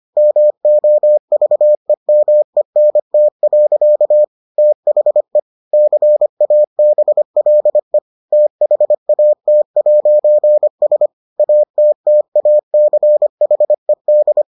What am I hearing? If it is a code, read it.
Morse code, heard as a keyed tone, about 25 wpm, 600 Hz. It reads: MOVEMENT. THE CABLE THAT'S ATTACHED